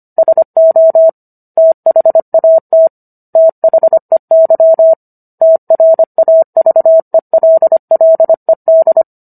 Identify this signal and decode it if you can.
Morse code, heard as a keyed tone, about 25 wpm, 650 Hz. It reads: SO THAT THEY TRAVELLED